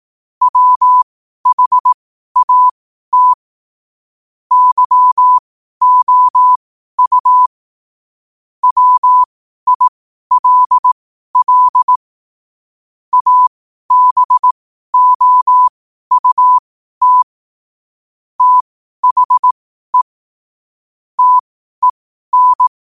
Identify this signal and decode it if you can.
Morse code, heard as a keyed tone, about 18 wpm, spaced as for 12 wpm, 1000 Hz. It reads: WHAT YOU WILL ABOUT THE TEN
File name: qotdfiles / 12wpm